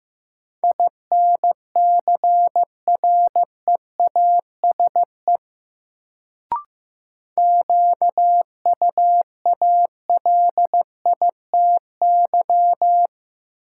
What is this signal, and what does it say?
Morse code, heard as a keyed tone, about 15 wpm, 700 Hz. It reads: INCREASE QUALITY
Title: Sentences from Top 1000 Words - Review Part 4 - New Word Drill - 15wpm